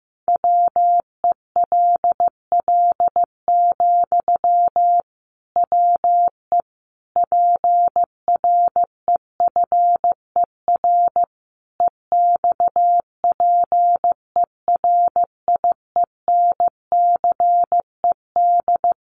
Morse code, heard as a keyed tone, about 15 wpm, 700 Hz. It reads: WELL, WE PREFER EXPERIENCED
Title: Sentences from Top 1100 Words 15wpm